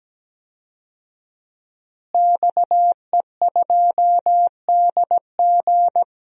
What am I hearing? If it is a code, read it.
Morse code, heard as a keyed tone, about 17 wpm, 700 Hz. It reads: XE2DG
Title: DXCC-100 Call Signs - 5 Characters - Repeated 2x - 17wpm